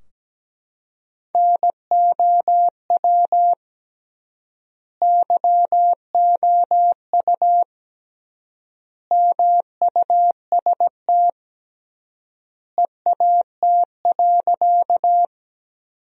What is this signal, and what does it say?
Morse code, heard as a keyed tone, about 17 wpm, 700 Hz. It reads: NOW YOU MUST EAT.
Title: 3x Word-Spacing - Sentences from Top 300 Words - 17wpm